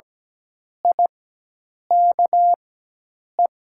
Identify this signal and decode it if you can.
Morse code, heard as a keyed tone, about 17 wpm, spaced as for 8 wpm, 700 Hz. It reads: IKE
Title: 4x Character Spacing - Top 300 Words - 17wpm